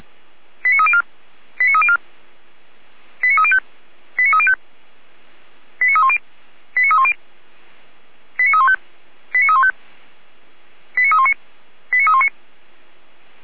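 Five-tone selective call (ZVEI), heard as quick runs of five tones, tones of about 70 ms. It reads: 89304 89304 89305 89305 89310 89310 89315 89315 89319 89319
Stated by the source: ZVEI five-tone selective call received by SDR.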